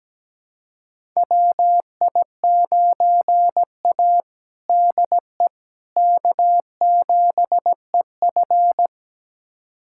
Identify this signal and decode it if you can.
Morse code, heard as a keyed tone, about 17 wpm, 700 Hz. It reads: WI9A DE K7EF
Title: USA Callsign DE Callsign 17wpm